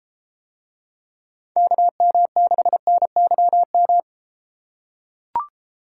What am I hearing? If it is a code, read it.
Morse code, heard as a keyed tone, about 33 wpm, 700 Hz. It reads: KM6NYM